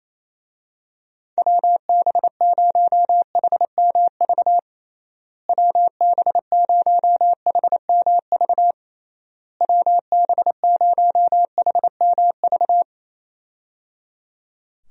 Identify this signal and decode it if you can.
Morse code, heard as a keyed tone, about 27 wpm, 700 Hz. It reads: WB0HMV WB0HMV WB0HMV
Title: USA Call Signs - 6 Characters - Repeated 3x - 28wpm